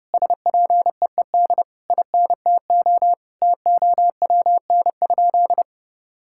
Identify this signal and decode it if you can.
Morse code, heard as a keyed tone, about 32 wpm, 700 Hz. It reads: SPEED INTO TOWN?
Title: Sentences from Top 1000 Words - Review Part 5 - 30wpm